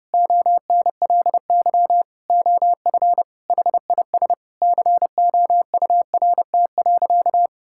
Morse code, heard as a keyed tone, about 32 wpm, 700 Hz. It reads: ONLY OF HIS COURT.
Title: Sentences from Top 1600 Words 30wpm